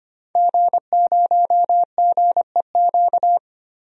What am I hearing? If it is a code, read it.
Morse code, heard as a keyed tone, about 25 wpm, 700 Hz. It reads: G0GEQ
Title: DXCC-100 Call Signs - 5 Characters - 25wpm